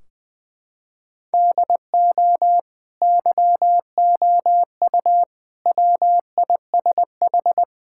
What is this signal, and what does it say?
Morse code, heard as a keyed tone, about 20 wpm, 700 Hz. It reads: DO YOU WISH
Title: Sentences from Top 1600 Words 20wpm